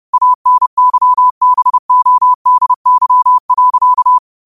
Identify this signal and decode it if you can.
Morse code, heard as a keyed tone, about 30 wpm, 1000 Hz. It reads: ANYBODY.
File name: qotdfiles / 30wpm